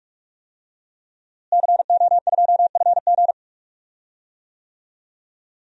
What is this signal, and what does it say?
Morse code, heard as a keyed tone, about 46 wpm, 700 Hz. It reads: CO2FG
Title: DXCC-100 Call Signs - 5 Characters - Repeated 2x - 45wpm